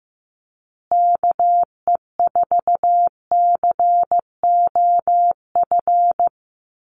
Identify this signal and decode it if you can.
Morse code, heard as a keyed tone, about 15 wpm, 700 Hz. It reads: KE4COF
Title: USA Call Signs Repeated 3x 15wpm